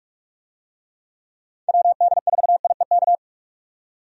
Morse code, heard as a keyed tone, about 47 wpm, 700 Hz. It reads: WD4IEK
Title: USA Call Signs - 6 Characters - 45wpm